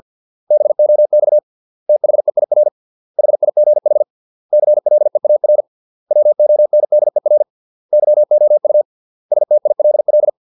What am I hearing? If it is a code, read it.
Morse code, heard as a keyed tone, about 50 wpm, 600 Hz. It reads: BOX THEIR HIGH CLEAR WONDER YOU STILL